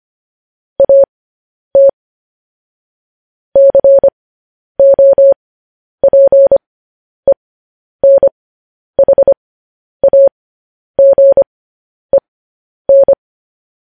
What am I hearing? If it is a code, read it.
Morse code, heard as a keyed tone, about 25 wpm, spaced as for 10 wpm, 550 Hz. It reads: AT COPENHAGEN
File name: podcast / News headlines-1x-25x10